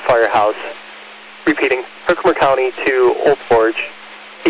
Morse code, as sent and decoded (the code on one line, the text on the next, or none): none